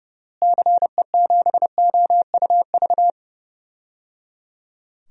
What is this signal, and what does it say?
Morse code, heard as a keyed tone, about 30 wpm, 700 Hz. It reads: CE7OUV
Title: DXCC-100 Call Signs - 6 Characters - Repeated 3x - 30wpm